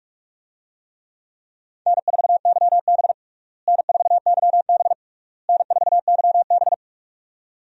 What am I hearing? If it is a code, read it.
Morse code, heard as a keyed tone, about 46 wpm, 700 Hz. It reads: N4YB N4YB N4YB
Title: USA Call Signs - 4 Characters - Repeated 3x - 45wpm